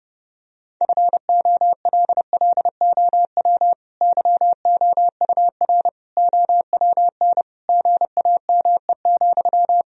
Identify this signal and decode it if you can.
Morse code, heard as a keyed tone, about 32 wpm, 700 Hz. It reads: FOLLOW YOUR OWN GAME,
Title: Sentences from Top 1000 Words - Review Part 3 - 30wpm